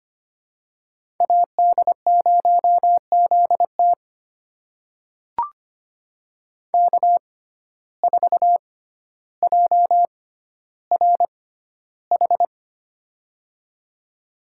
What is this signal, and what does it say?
Morse code, heard as a keyed tone, about 25 wpm, spaced as for 13 wpm, 700 Hz. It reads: AD0ZT K4JRH
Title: USA Call Signs - 5 Characters - 6x Character Spacing - 25wpm